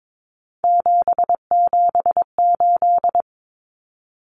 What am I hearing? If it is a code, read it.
Morse code, heard as a keyed tone, about 22 wpm, 700 Hz. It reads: 778
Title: Numbers - 3-Digit 22wpm